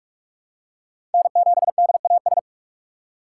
Morse code, heard as a keyed tone, about 43 wpm, 700 Hz. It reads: N7DAS
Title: USA Call Signs - 5 Characters - 3x Character Spacing - 45wpm